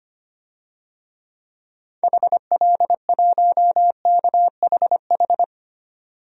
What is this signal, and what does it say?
Morse code, heard as a keyed tone, about 25 wpm, 700 Hz. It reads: HL1KHH
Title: DXCC-100 Call Signs - 6 Characters - 25wpm